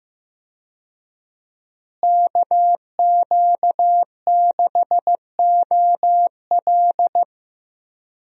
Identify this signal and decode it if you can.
Morse code, heard as a keyed tone, about 15 wpm, 700 Hz. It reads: KQ6OL
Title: USA Call Signs - 5 Characters - Repeated 3x - 15wpm